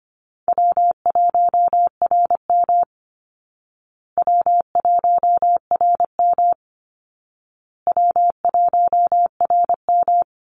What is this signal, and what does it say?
Morse code, heard as a keyed tone, about 25 wpm, 700 Hz. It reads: W1RM W1RM W1RM
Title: Easy Call Signs Repeated 3x - 25wpm